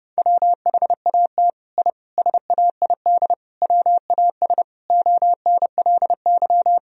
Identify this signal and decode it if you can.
Morse code, heard as a keyed tone, about 30 wpm, 700 Hz. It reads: WHAT I SAID WAS ONLY
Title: Sentences from Top 1000 Words - Review Part 4 - 30wpm